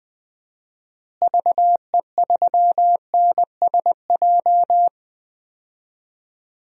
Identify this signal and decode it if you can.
Morse code, heard as a keyed tone, about 19 wpm, 700 Hz. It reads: VE3NSJ
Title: DXCC-100 Call Signs - 6 Characters - Repeated 3x - 20wpm